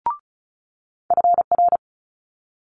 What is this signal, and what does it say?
Morse code, heard as a keyed tone, about 36 wpm, 700 Hz. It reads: FR
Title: Multi-letter Phonemes Quick Fire 35wpm